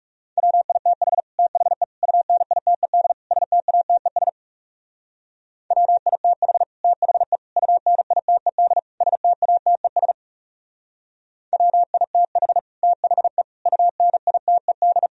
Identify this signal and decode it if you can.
Morse code, heard as a keyed tone, about 39 wpm, 700 Hz. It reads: WITH THE UNITED STATES WITH THE UNITED STATES WITH THE UNITED